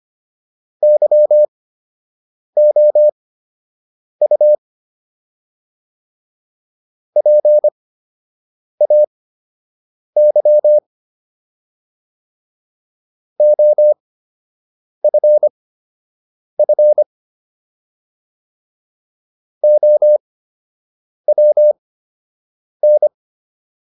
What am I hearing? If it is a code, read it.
Morse code, heard as a keyed tone, about 25 wpm, spaced as for 7 wpm, 600 Hz. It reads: YOU PAY OFF OWN